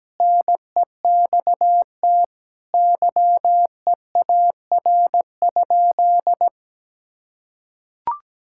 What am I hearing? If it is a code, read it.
Morse code, heard as a keyed tone, about 17 wpm, 700 Hz. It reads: NEXT YEAR?
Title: Sentences from Top 1000 Words - Review Part 5 - 17wpm